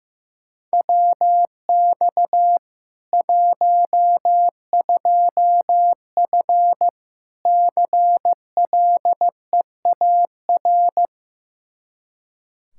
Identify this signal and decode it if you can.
Morse code, heard as a keyed tone, about 15 wpm, 700 Hz. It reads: WX 12F CLEAR
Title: Practice QSO - Exchange 2 - Info - 15wpm